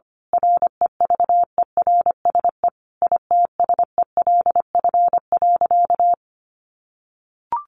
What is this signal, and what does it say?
Morse code, heard as a keyed tone, about 25 wpm, 700 Hz. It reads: REVERSE ITSELF.
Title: A Sound Of Thunder 25wpm